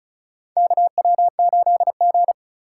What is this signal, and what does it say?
Morse code, heard as a keyed tone, about 34 wpm, 700 Hz. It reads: KW8G
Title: USA Call Signs - 4 Characters - 3x Character Spacing - 35wpm